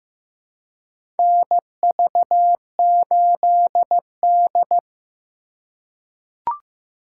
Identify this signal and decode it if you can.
Morse code, heard as a keyed tone, about 15 wpm, 700 Hz. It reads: NV8D
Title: USA Call Signs - 4 Characters - 3x Character Spacing - 15wpm